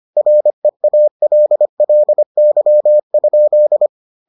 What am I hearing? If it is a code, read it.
Morse code, heard as a keyed tone, about 25 wpm, 600 Hz. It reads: REALLY?